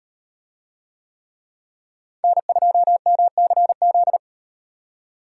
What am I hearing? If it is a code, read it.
Morse code, heard as a keyed tone, about 35 wpm, 700 Hz. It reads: N2MCZ